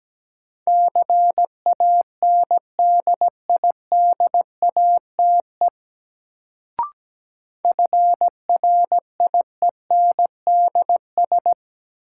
Morse code, heard as a keyed tone, about 17 wpm, 700 Hz. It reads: CANDIDATE FRIENDS
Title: Sentences from Top 1600 Words - New Word Drill Alt - 17wpm